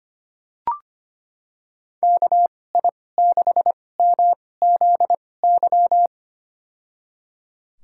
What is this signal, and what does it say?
Morse code, heard as a keyed tone, about 25 wpm, spaced as for 18 wpm, 700 Hz. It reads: KI6MZY
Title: USA Call Signs - 2x Character Spacing - 25wpm